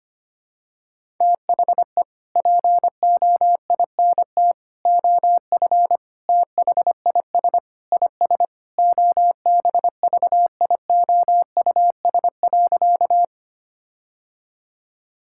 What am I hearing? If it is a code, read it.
Morse code, heard as a keyed tone, about 25 wpm, 700 Hz. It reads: THE POINT OF THIS IS OBVIOUS.
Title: Sentences from Top 1600 Words 25wpm